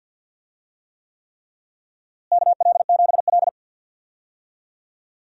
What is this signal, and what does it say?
Morse code, heard as a keyed tone, about 47 wpm, 700 Hz. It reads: KL7L
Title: DXCC-100 Call Signs - 4 Characters - Repeated 3x - 50wpm